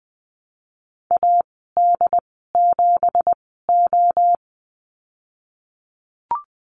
Morse code, heard as a keyed tone, about 20 wpm, spaced as for 14 wpm, 700 Hz. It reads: AD7O
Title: 173 Lesson 28g - Number 9 - Call Signs 20wpm